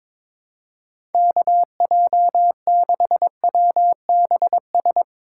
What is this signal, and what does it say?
Morse code, heard as a keyed tone, about 22 wpm, 700 Hz. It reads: KJ6WBS
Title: USA Call Signs - 6 Characters - 4x Character Spacing - 22wpm